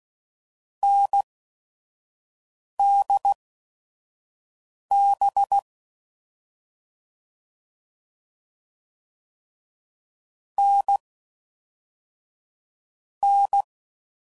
Morse code, heard as a keyed tone, about 16 wpm, spaced as for 5 wpm, 800 Hz. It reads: NDB N N
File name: Mixed / Short Runs / awj1ndb6gz7